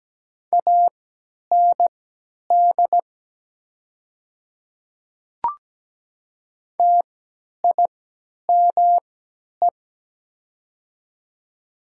Morse code, heard as a keyed tone, about 17 wpm, spaced as for 10 wpm, 700 Hz. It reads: AND TIME
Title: 3x Character Spacing - Top 200 Words - 17wpm